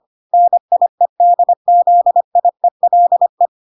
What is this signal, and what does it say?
Morse code, heard as a keyed tone, about 25 wpm, 700 Hz. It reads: NIEDZIELE